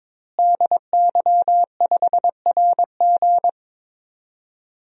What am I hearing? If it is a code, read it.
Morse code, heard as a keyed tone, about 22 wpm, 700 Hz. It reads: DY5RG
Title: DXCC-100 Call Signs - 5 Characters - Repeated 2x - 22wpm